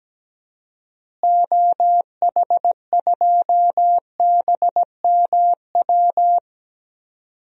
Morse code, heard as a keyed tone, about 17 wpm, 700 Hz. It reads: OH2BMW
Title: DXCC-100 Call Signs - 6 Characters - Repeated 2x - 17wpm